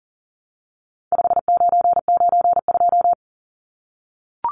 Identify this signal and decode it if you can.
Morse code, heard as a keyed tone, about 40 wpm, 700 Hz. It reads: 5992
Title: Numbers - 4-Digit 40wpm